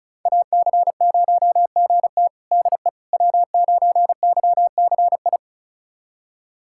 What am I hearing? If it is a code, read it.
Morse code, heard as a keyed tone, about 36 wpm, 700 Hz. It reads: AC0GT DE W9YCI
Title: USA Callsign DE Callsign - 5 Characters - 35wpm